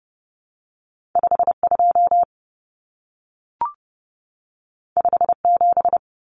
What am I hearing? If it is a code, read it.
Morse code, heard as a keyed tone, about 30 wpm, 700 Hz. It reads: 52 57
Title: Common Cut Numbers - 2-Digit 30wpm